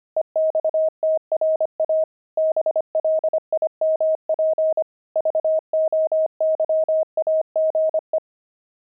Morse code, heard as a keyed tone, about 25 wpm, 630 Hz. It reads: EXTRA BLIMP VOYAGE
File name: pangram-fs-25-cs-25-blimp